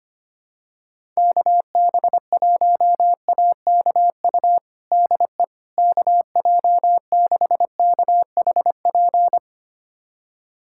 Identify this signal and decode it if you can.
Morse code, heard as a keyed tone, about 25 wpm, 700 Hz. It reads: KB1AKU DE KJ6KHP
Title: USA Callsign DE Callsign - 6 Characters - 25wpm